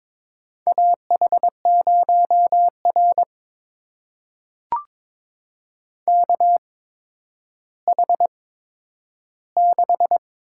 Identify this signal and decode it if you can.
Morse code, heard as a keyed tone, about 22 wpm, spaced as for 11 wpm, 700 Hz. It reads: AH0R KH6
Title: USA Call Signs - 8x Character Spacing - 22wpm